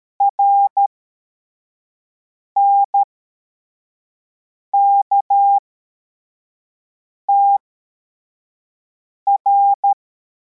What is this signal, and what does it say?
Morse code, heard as a keyed tone, about 13 wpm, spaced as for 4 wpm, 800 Hz. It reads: RNKTR